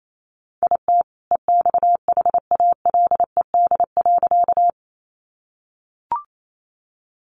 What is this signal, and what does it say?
Morse code, heard as a keyed tone, about 28 wpm, 700 Hz. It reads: IT EXHALED.
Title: A Sound Of Thunder 28wpm